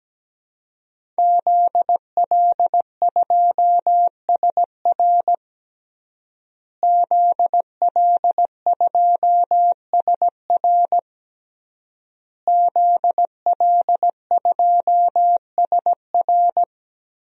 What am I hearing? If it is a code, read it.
Morse code, heard as a keyed tone, about 17 wpm, 700 Hz. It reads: ZL2SR ZL2SR ZL2SR
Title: DXCC-100 Call Signs - 5 Characters - Repeated 3x - 17wpm